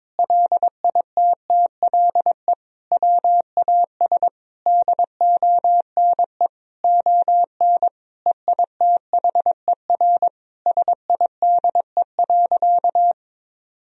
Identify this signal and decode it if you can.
Morse code, heard as a keyed tone, about 22 wpm, 700 Hz. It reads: LITTLE WAS DONE ON EITHER SIDE.
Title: Sentences from Top 1000 Words - Review Part 4 - 22wpm